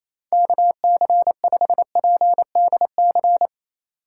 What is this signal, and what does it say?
Morse code, heard as a keyed tone, about 28 wpm, 700 Hz. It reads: KC5PDC
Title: USA Call Signs - 2x Character Spacing - 28wpm